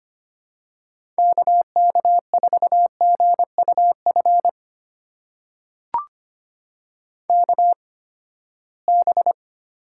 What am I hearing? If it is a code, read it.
Morse code, heard as a keyed tone, about 25 wpm, spaced as for 17 wpm, 700 Hz. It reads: KK4GUF KB
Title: USA Call Signs - 6 Characters - 8x Character Spacing - 25wpm